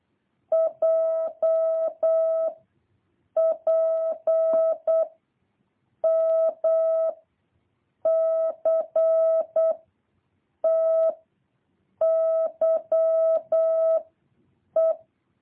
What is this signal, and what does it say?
Morse code, heard as a keyed tone, about 8 wpm, 650 Hz. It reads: J P M C T Y E
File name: jh8cbh / 2023cw